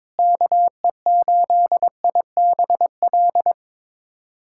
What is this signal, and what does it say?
Morse code, heard as a keyed tone, about 22 wpm, 700 Hz. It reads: KE8IBL
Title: USA Callsign DE Callsign - 6 Characters - 22wpm